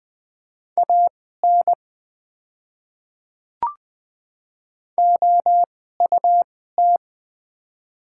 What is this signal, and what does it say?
Morse code, heard as a keyed tone, about 20 wpm, spaced as for 14 wpm, 690 Hz. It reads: AN OUT
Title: 2x Character Spacing - Top 100 Words - 20wpm